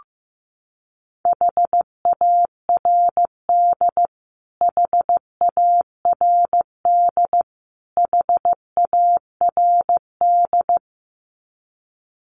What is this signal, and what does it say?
Morse code, heard as a keyed tone, about 15 wpm, 700 Hz. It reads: HARD HARD HARD
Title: Top 500 Words Repeated 15wpm